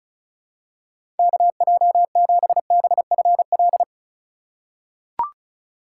Morse code, heard as a keyed tone, about 34 wpm, 700 Hz. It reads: KJ7BFL
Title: USA Call Signs - 6 Characters - 2x Character Spacing - 35wpm